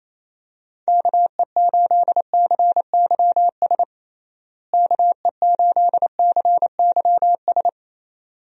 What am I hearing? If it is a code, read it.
Morse code, heard as a keyed tone, about 28 wpm, 700 Hz. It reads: KE8CYS KE8CYS
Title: USA Call Signs - 6 Characters - Repeated 2x - 28wpm